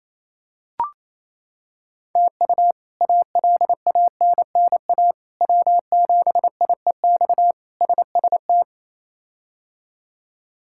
Morse code, heard as a keyed tone, about 28 wpm, 700 Hz. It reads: TU ALANNA W7IEX SST
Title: K1USN SST QSO Practice - 28wpm